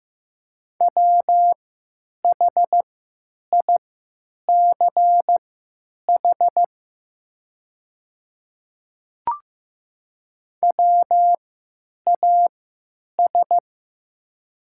Morse code, heard as a keyed tone, about 15 wpm, spaced as for 9 wpm, 700 Hz. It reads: WHICH WAS
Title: 3x Character Spacing - Top 100 Words - 15wpm